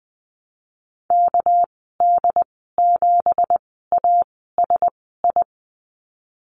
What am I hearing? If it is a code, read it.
Morse code, heard as a keyed tone, about 20 wpm, spaced as for 15 wpm, 700 Hz. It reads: KD7ASI
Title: 219 Lesson 35g - Number 0 - Call Signs 20wpm